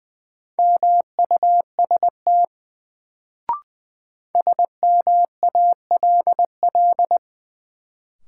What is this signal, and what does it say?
Morse code, heard as a keyed tone, about 20 wpm, 700 Hz. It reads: MUST SMALL
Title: Sets of 2 Words - New Word Drill - 20wpm